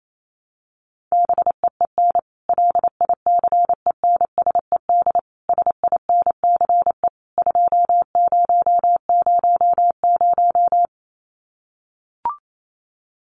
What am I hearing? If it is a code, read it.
Morse code, heard as a keyed tone, about 28 wpm, 700 Hz. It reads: BEEN LICENSED SINCE 2000
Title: Common QSO Phrases 28wpm